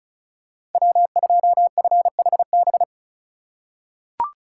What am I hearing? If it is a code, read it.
Morse code, heard as a keyed tone, about 36 wpm, 700 Hz. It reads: W2FHB
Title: USA Call Signs - 5 Characters - 4x Character Spacing - 35wpm